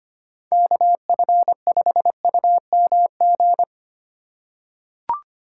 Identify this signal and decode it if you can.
Morse code, heard as a keyed tone, about 25 wpm, 700 Hz. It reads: KF5UMG
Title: USA Call Signs - 6 Characters - 2x Character Spacing - 25wpm